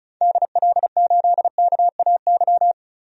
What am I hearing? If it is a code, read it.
Morse code, heard as a keyed tone, about 34 wpm, 700 Hz. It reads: DL8KAY
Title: DXCC-100 Call Signs - 6 Characters - 35wpm